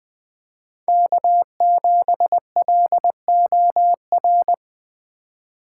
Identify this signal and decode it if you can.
Morse code, heard as a keyed tone, about 20 wpm, 700 Hz. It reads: K7LOR
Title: USA Call Signs - 5 Characters - 4x Character Spacing - 20wpm